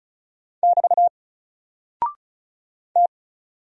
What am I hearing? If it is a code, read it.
Morse code, heard as a keyed tone, about 34 wpm, 700 Hz. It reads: <BT> T
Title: Single Character (Letter:Number:Punctuation:Prosign) Alt 35wpm